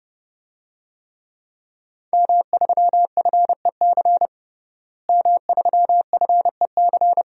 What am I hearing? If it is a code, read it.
Morse code, heard as a keyed tone, about 30 wpm, 700 Hz. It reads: M3FEC M3FEC
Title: DXCC-100 Call Signs - 5 Characters - Repeated 2x - 30wpm